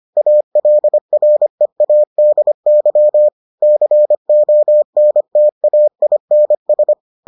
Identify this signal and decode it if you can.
Morse code, heard as a keyed tone, about 25 wpm, 600 Hz. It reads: ALREADY CONTAINS